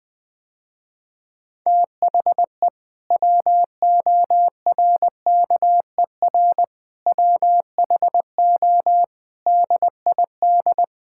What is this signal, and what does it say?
Morse code, heard as a keyed tone, about 20 wpm, 700 Hz. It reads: THE WORKER WHO DID
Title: Sentences with 7-Word Phrases - Spoken Beginning - 20wpm